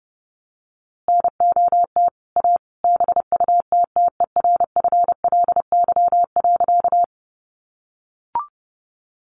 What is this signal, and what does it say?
Morse code, heard as a keyed tone, about 30 wpm, 700 Hz. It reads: NOT A BUTTERFLY.
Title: A Sound Of Thunder 30wpm